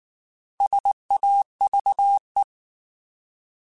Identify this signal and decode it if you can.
Morse code, heard as a keyed tone, about 19 wpm, 800 Hz. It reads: SAVE
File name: Alphabetical / Alphabetical mp3s / Ss @18